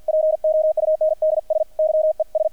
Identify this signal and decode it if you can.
Morse code, heard as a keyed tone, about 50 wpm, 630 Hz, with radio noise. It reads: POUNDS YES
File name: archived / CRSnet